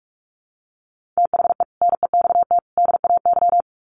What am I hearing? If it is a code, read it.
Morse code, heard as a keyed tone, about 44 wpm, 700 Hz. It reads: THE NEXT DAY
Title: Sets of 3 Words 45wpm